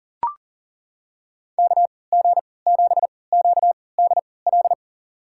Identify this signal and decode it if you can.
Morse code, heard as a keyed tone, about 40 wpm, spaced as for 23 wpm, 700 Hz. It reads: KG7QDL
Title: USA Call Signs - 6 Characters - 3x Character Spacing - 40wpm